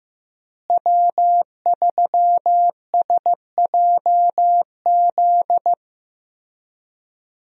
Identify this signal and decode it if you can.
Morse code, heard as a keyed tone, about 15 wpm, 700 Hz. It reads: W3SJZ
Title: USA Call Signs - 5 Characters - Repeated 2x - 15wpm